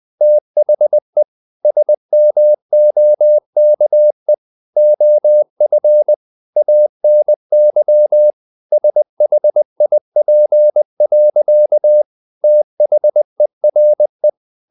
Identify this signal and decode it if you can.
Morse code, heard as a keyed tone, about 20 wpm, 600 Hz. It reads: THE SMOKE OF ANY SHIP. THERE
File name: the old man and the sea part 2 20wpm